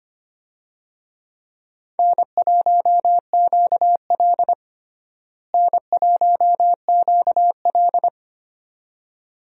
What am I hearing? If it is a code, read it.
Morse code, heard as a keyed tone, about 25 wpm, 700 Hz. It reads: N1QL N1QL
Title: USA Call Signs - 4 Characters - Repeated 2x - 25wpm